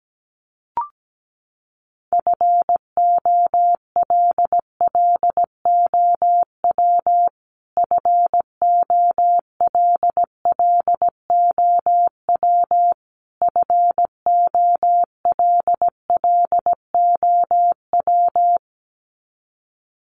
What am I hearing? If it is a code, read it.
Morse code, heard as a keyed tone, about 17 wpm, 700 Hz. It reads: FOLLOW FOLLOW FOLLOW
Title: Top 200 Words Repeated 17wpm